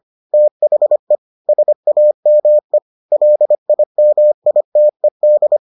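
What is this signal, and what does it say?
Morse code, heard as a keyed tone, about 25 wpm, 600 Hz. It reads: THE SAME LIMITED
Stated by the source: Morse edition of the Foundations of Amateur Radio episodes.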